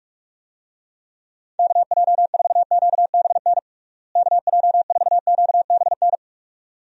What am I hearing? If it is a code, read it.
Morse code, heard as a keyed tone, about 42 wpm, 700 Hz. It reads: KJ4QBN KJ4QBN
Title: USA Call Signs - 6 Characters - Repeated 2x - 45wpm